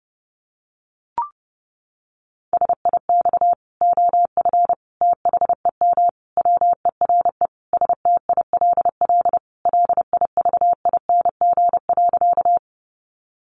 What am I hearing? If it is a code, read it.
Morse code, heard as a keyed tone, about 30 wpm, 700 Hz. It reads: SIX OF THEM WERE STILL LIVING.
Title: Sentences from Top 1100 Words 30wpm